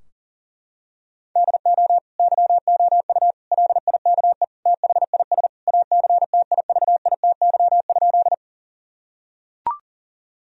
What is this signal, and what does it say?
Morse code, heard as a keyed tone, about 40 wpm, 700 Hz. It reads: DO YOU LIKE THIS ACTIVITY?
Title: Sentences from Top 1000 Words - Review Part 4 - 40wpm